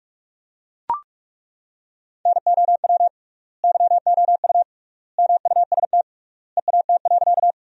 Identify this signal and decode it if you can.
Morse code, heard as a keyed tone, about 41 wpm, 700 Hz. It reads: NOW YOU MUST EAT.
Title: 3x Word-Spacing - Sentences from Top 300 Words - 45wpm